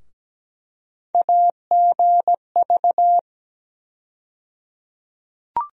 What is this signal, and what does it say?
Morse code, heard as a keyed tone, about 17 wpm, 700 Hz. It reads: AGV
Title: USA Call Sign Postfixes - 3 Letters 17wpm